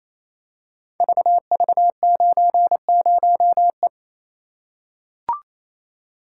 Morse code, heard as a keyed tone, about 27 wpm, 700 Hz. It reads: VV90E
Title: Code Groups - 5 Letter:Numbers - 28wpm